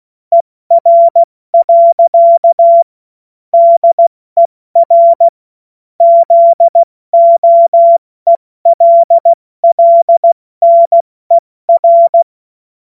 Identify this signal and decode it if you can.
Morse code, heard as a keyed tone, about 16 wpm, 680 Hz. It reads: ER. DER ZOELLNER
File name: Morsekurs / Witze / Witz 27. Januar 2026 70 F80